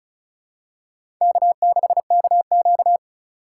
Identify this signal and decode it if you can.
Morse code, heard as a keyed tone, about 36 wpm, 700 Hz. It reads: K6KQ
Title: USA Call Signs - 4 Characters - 3x Character Spacing - 35wpm